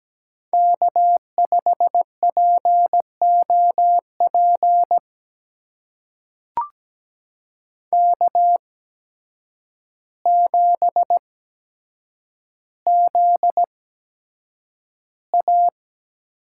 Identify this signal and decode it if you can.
Morse code, heard as a keyed tone, about 17 wpm, spaced as for 8 wpm, 700 Hz. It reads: K5POP K7ZA
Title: USA Call Signs - 5 Characters - 8x Character Spacing - 17wpm